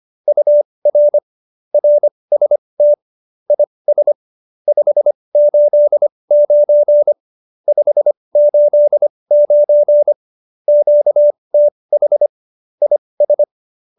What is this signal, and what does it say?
Morse code, heard as a keyed tone, about 25 wpm, spaced as for 20 wpm, 590 Hz. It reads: UR RST IS 589 589 QTH IS
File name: QSO303 20